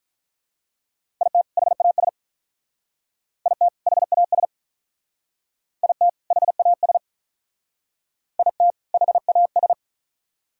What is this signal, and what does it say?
Morse code, heard as a keyed tone, about 45 wpm, 700 Hz. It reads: IT HAS IT HAS IT HAS IT HAS